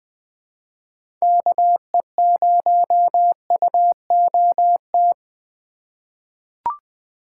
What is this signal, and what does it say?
Morse code, heard as a keyed tone, about 20 wpm, 700 Hz. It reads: KE0UOT
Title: USA Call Signs - 6 Characters - 3x Character Spacing - 20wpm